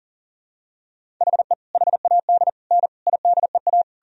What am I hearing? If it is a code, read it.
Morse code, heard as a keyed tone, about 40 wpm, 700 Hz. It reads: HE HAD N IDEA
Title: Sentences from Top 300 Words and Abbreviations 40wpm